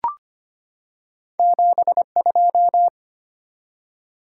Morse code, heard as a keyed tone, about 25 wpm, 700 Hz. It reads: 72
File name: Two Letter-Number Combinations - 25wpm